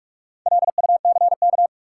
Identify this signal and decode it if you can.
Morse code, heard as a keyed tone, about 46 wpm, 700 Hz. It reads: LUCK
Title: Sentences from Top 1500 Words - New Word Drill - 45wpm